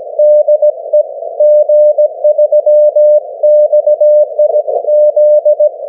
Morse code, heard as a keyed tone, about 17 wpm, 600 Hz. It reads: DE G3XIZ
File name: GW3UEP / 16Dec12 9AM G3XIZ 501kHz